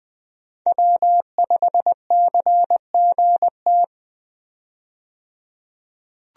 Morse code, heard as a keyed tone, about 21 wpm, 700 Hz. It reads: W5CGT
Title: USA Call Signs - 5 Characters - 20wpm